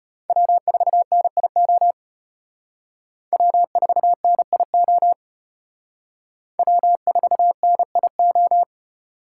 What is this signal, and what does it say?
Morse code, heard as a keyed tone, about 32 wpm, 700 Hz. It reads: W4NIO W4NIO W4NIO